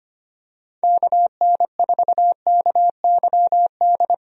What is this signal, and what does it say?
Morse code, heard as a keyed tone, about 25 wpm, 700 Hz. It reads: KN4KYD
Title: USA Call Signs - 6 Characters - Repeated 2x - 25wpm